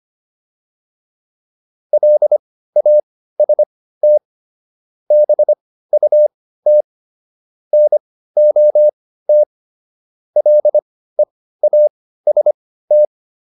Morse code, heard as a keyed tone, about 25 wpm, spaced as for 15 wpm, 600 Hz. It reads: LAST BUT NOT LEAST